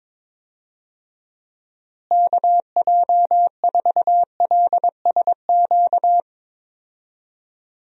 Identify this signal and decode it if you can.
Morse code, heard as a keyed tone, about 22 wpm, 700 Hz. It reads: KJ4LSQ
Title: USA Call Signs - 6 Characters - Repeated 2x - 22wpm